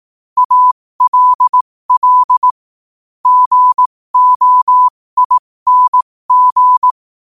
Morse code, heard as a keyed tone, about 18 wpm, spaced as for 15 wpm, 1000 Hz. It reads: ALL GOING
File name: qotdfiles / 15wpm